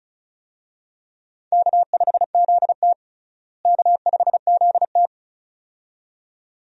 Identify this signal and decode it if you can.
Morse code, heard as a keyed tone, about 36 wpm, 700 Hz. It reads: K5ZT K5ZT
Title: USA Call Signs - 4 Characters - Repeated 2x - 35wpm